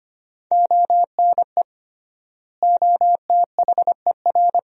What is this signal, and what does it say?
Morse code, heard as a keyed tone, about 25 wpm, 700 Hz. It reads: ONE OTHER
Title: 3x Word-Spacing - Sentences from Top 400 Words - 25wpm